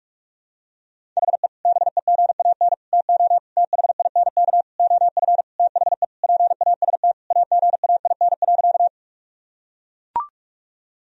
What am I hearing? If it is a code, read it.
Morse code, heard as a keyed tone, about 41 wpm, 700 Hz. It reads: HE BEGAN TO THINK OF THE PAST AGAIN.
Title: Sentences from Top 1000 Words - Review Part 3 - 45wpm